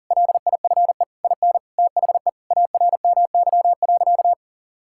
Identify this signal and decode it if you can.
Morse code, heard as a keyed tone, about 40 wpm, 700 Hz. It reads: LIFE IN THE ARMY.
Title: Sentences from Top 1000 Words - Review Part 5 - 40wpm